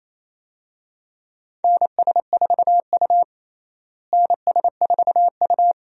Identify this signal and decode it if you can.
Morse code, heard as a keyed tone, about 28 wpm, 700 Hz. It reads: NS4U NS4U
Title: USA Call Signs - 4 Characters - Repeated 3x - 28wpm